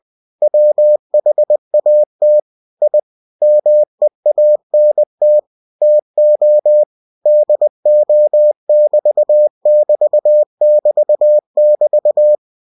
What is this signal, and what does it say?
Morse code, heard as a keyed tone, about 20 wpm, 600 Hz. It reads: WHAT I MEANT TO DO<BT><BT><BT><BT>
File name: cw-books / winnie-the-pooh / winnie-20wpm-page-0003